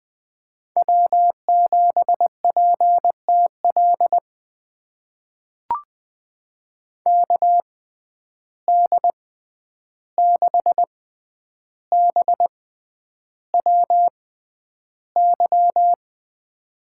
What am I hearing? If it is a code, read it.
Morse code, heard as a keyed tone, about 20 wpm, spaced as for 10 wpm, 700 Hz. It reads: W7PTL KD6BWY
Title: USA Call Signs - 6x Character Spacing - 20wpm